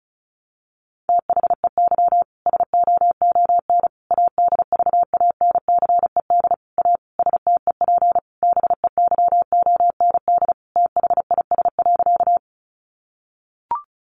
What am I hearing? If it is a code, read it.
Morse code, heard as a keyed tone, about 34 wpm, 700 Hz. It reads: THEY SOON ADVANCED A STEP BEYOND THIS.
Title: Sentences from Top 1200 Words 35wpm